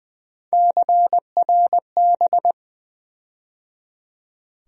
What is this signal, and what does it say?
Morse code, heard as a keyed tone, about 20 wpm, 700 Hz. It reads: CRB
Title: USA Call Sign Postfixes - 3 Letters 20wpm